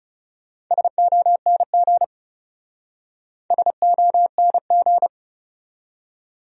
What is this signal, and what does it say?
Morse code, heard as a keyed tone, about 40 wpm, 700 Hz. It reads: SONG SONG